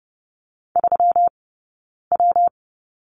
Morse code, heard as a keyed tone, about 30 wpm, 700 Hz. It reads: 3 W
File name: Lesson 20d - Number 3 - Two Characters 30wpm